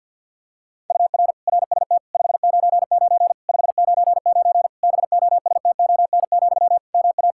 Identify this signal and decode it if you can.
Morse code, heard as a keyed tone, about 50 wpm, 700 Hz. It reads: UR RST 599 599 BOSTON, MA
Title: Practice QSO - Exchange 1 - Beginning - 50wpm